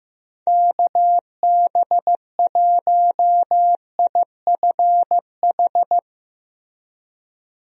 Morse code, heard as a keyed tone, about 15 wpm, 700 Hz. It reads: KB1IFH
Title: USA Call Signs - 6 Characters - Repeated 3x - 15wpm